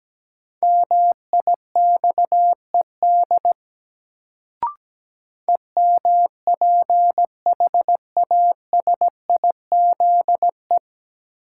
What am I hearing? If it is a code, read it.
Morse code, heard as a keyed tone, about 17 wpm, 700 Hz. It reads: MIXED EMPHASIZE
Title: Sentences from Top 1500 Words - New Word Drill - 17wpm